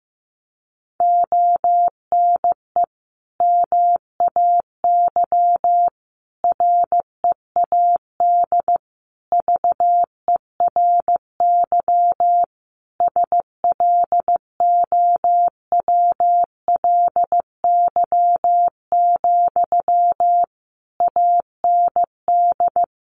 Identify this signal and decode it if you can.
Morse code, heard as a keyed tone, about 15 wpm, 700 Hz. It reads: ONE MAY READ VERY SLOWLY, AND
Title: Sentences from Top 1100 Words 15wpm